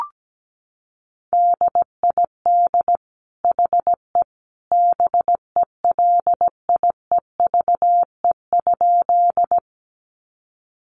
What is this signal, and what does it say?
Morse code, heard as a keyed tone, about 17 wpm, 700 Hz. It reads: DID HE BELIEVE?
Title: Sentences from Top 1000 Words 17wpm